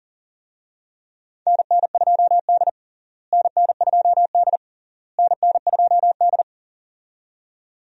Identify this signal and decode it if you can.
Morse code, heard as a keyed tone, about 40 wpm, 700 Hz. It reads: NN2D NN2D NN2D
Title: USA Call Signs - 4 Characters - Repeated 3x - 40wpm